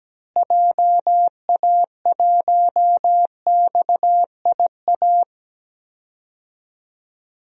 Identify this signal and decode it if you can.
Morse code, heard as a keyed tone, about 17 wpm, 700 Hz. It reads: JA1XIA
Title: DXCC-100 Call Signs - 6 Characters - Repeated 2x - 17wpm